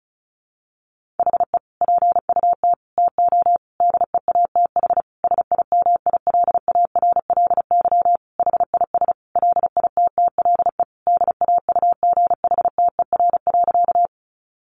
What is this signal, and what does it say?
Morse code, heard as a keyed tone, about 34 wpm, 700 Hz. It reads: HE PUT TO DEATH SIMILARLY HIS LITTLE DAUGHTER.
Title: Sentences from Top 1500 Words 35wpm